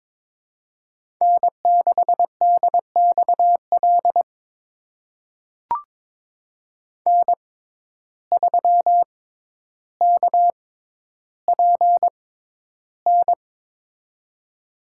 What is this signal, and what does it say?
Morse code, heard as a keyed tone, about 21 wpm, spaced as for 11 wpm, 700 Hz. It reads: N6DXL N3KPN
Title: USA Call Signs - 6x Character Spacing - 22wpm